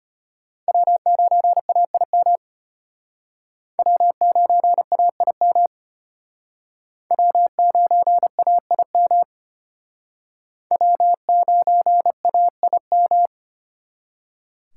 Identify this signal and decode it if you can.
Morse code, heard as a keyed tone, about 32 wpm, 700 Hz. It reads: W9AIM W9AIM W9AIM W9AIM